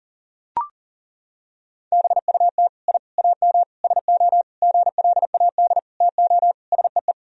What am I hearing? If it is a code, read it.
Morse code, heard as a keyed tone, about 40 wpm, 700 Hz. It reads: BUT I AM SO GLAD TO SEE
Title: Sentences from Top 1000 Words - Review Part 2 - 40wpm